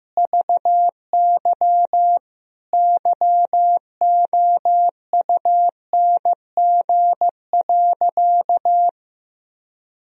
Morse code, heard as a keyed tone, about 15 wpm, 700 Hz. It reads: VY YOUNG.
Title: Sentences from Top 400 Words and Abbreviations 15wpm